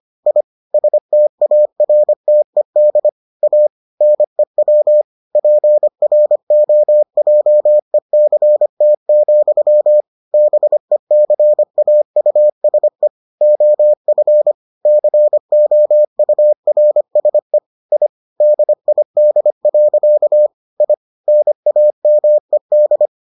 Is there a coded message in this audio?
Yes: Morse code from 0.3 to 23.1 s, heard as a keyed tone, about 25 wpm, 600 Hz. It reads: I STARTED A NEW PROJECT, BECAUSE OF COURSE I DID. I NAMED